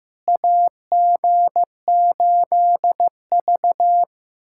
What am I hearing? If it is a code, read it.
Morse code, heard as a keyed tone, about 15 wpm, 700 Hz. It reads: AG8V
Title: USA Callsign DE Callsign - 4 Characters - 15wpm